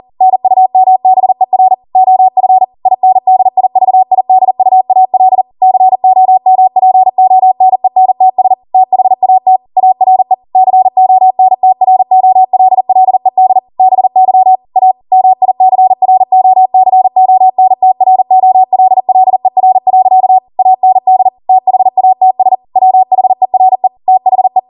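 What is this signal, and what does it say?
Morse code, heard as a keyed tone, about 40 wpm, 750 Hz. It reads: NUMBER OF INDIVIDUAL COMPONENTS THAT ARE CONTROLLED BY A MICROCONTROLLER. AND THATS WHERE THE
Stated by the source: ARRL 40 WPM Morse code practice transmission.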